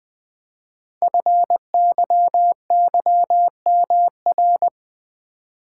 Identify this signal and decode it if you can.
Morse code, heard as a keyed tone, about 20 wpm, 700 Hz. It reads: FYYMR
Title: Code Groups - 5 Letter:Numbers - 20wpm